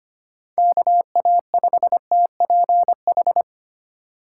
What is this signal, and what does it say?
Morse code, heard as a keyed tone, about 25 wpm, 700 Hz. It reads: KA5TPH
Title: USA Call Signs - 6 Characters - 2x Character Spacing - 25wpm